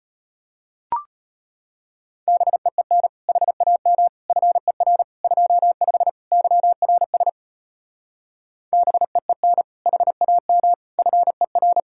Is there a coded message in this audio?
Yes: Morse code from 2.3 to 11.8 s, heard as a keyed tone, about 33 wpm, 700 Hz. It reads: BEEN HAM FER 25 YRS BEEN HAM FER